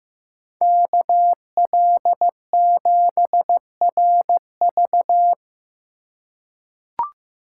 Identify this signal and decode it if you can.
Morse code, heard as a keyed tone, about 15 wpm, 700 Hz. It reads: KL7RV
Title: USA Call Signs - 5 Characters - 3x Character Spacing - 15wpm